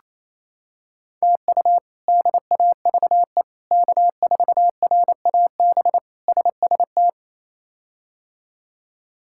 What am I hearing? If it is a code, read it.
Morse code, heard as a keyed tone, about 28 wpm, 700 Hz. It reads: TU DAVE K4RAB SST
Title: K1USN SST QSO Practice - 28wpm